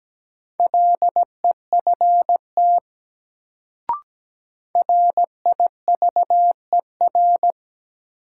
Morse code, heard as a keyed tone, about 17 wpm, 700 Hz. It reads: LEFT RIVER
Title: Sentences from Top 300 Words - New Word Drill - 17wpm